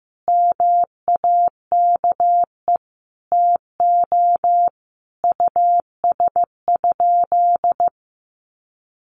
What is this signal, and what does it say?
Morse code, heard as a keyed tone, about 14 wpm, 700 Hz. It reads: MAKE TO US?
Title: Sentences from Top 1400 Words 15wpm